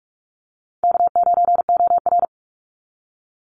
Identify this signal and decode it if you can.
Morse code, heard as a keyed tone, about 46 wpm, 700 Hz. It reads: K9OR
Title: Call Signs - 45wpm